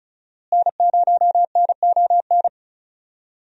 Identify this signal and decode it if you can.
Morse code, heard as a keyed tone, about 36 wpm, 700 Hz. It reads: N0NON
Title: USA Call Signs - 5 Characters - Repeated 3x - 35wpm